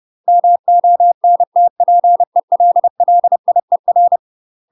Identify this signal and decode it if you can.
Morse code, heard as a keyed tone, about 30 wpm, 700 Hz. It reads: MONTPELLIER
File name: UK MASTER QSOs TRAINING/30WPM